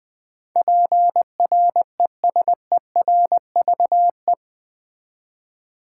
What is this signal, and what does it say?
Morse code, heard as a keyed tone, about 20 wpm, 700 Hz. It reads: PRESERVE
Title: Eight Letter Words - 20wpm